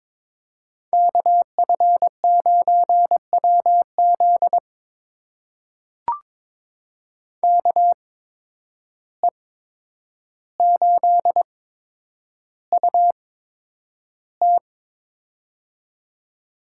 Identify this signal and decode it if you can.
Morse code, heard as a keyed tone, about 23 wpm, spaced as for 10 wpm, 700 Hz. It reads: KF9WZ KE8UT
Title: USA Call Signs - 5 Characters - 8x Character Spacing - 22wpm